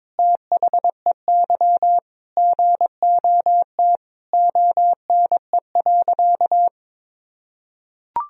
Morse code, heard as a keyed tone, about 22 wpm, 700 Hz. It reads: THEY GOT ONE.
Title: Sentences from Top 300 Words and Abbreviations 22wpm